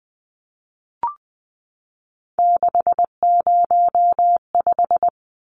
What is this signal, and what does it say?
Morse code, heard as a keyed tone, about 20 wpm, 700 Hz. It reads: 605
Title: Numbers - 3-Digit 20wpm